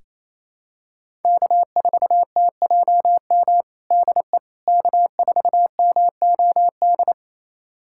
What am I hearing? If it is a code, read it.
Morse code, heard as a keyed tone, about 27 wpm, 700 Hz. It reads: K4TJM DE K4MOD
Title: USA Callsign DE Callsign - 5 Characters - 28wpm